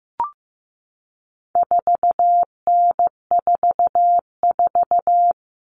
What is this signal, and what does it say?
Morse code, heard as a keyed tone, about 15 wpm, 700 Hz. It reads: 4N44
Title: All Cut Numbers - 4-Digit 15wpm